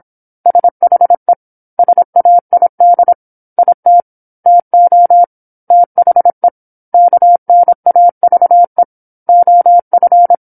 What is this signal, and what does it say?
Morse code, heard as a keyed tone, about 27 wpm, 700 Hz. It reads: SHE SAID IT TO THE KNAVE OF
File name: 08-Chapter-08 26 WPM